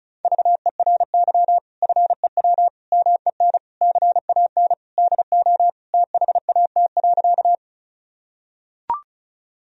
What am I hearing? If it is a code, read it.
Morse code, heard as a keyed tone, about 34 wpm, 700 Hz. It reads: VERY FEW MEN CAN DO THAT.
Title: Sentences from Top 1000 Words - Review Part 1 - 35wpm